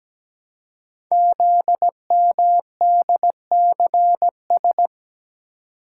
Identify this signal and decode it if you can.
Morse code, heard as a keyed tone, about 17 wpm, 700 Hz. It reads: ZMDCS
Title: Code Groups - 5 Letters - 17wpm